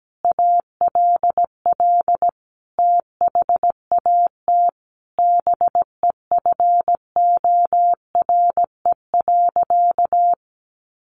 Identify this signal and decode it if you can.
Morse code, heard as a keyed tone, about 17 wpm, 700 Hz. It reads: ALL THAT BEFORE.
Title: Sentences from Top 400 Words 17wpm